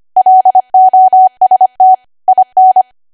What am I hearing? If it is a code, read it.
Morse code, heard as a keyed tone, about 25 wpm, 750 Hz. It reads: LOST IN
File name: Archive / 25 WPM / 25WPM